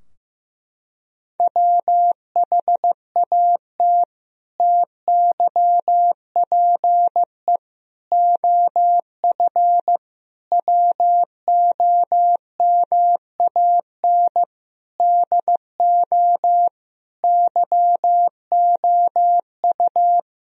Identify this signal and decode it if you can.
Morse code, heard as a keyed tone, about 15 wpm, 700 Hz. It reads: WHAT TYPE OF WOMAN DO YOU
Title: Sentences from Top 1000 Words - Review Part 3 - 15wpm